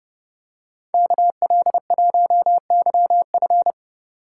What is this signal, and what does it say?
Morse code, heard as a keyed tone, about 28 wpm, 700 Hz. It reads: KL1YF